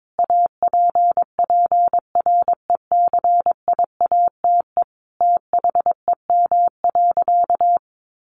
Morse code, heard as a keyed tone, about 22 wpm, 700 Hz. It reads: APPRECIATE THEM.
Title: Sentences from Top 1300 Words 22wpm